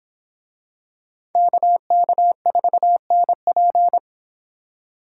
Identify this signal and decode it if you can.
Morse code, heard as a keyed tone, about 25 wpm, 700 Hz. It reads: KK4NP